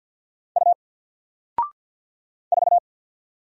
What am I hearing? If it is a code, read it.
Morse code, heard as a keyed tone, about 50 wpm, 700 Hz. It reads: U 4
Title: Single Letter-Number Alt 50wpm